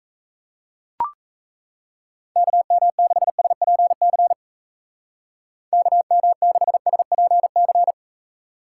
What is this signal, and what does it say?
Morse code, heard as a keyed tone, about 36 wpm, 700 Hz. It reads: KM6SPC KM6SPC